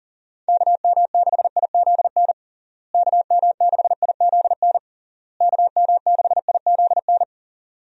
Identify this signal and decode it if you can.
Morse code, heard as a keyed tone, about 40 wpm, 700 Hz. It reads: KM6IZN KM6IZN KM6IZN
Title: USA Call Signs - 6 Characters - Repeated 3x - 40wpm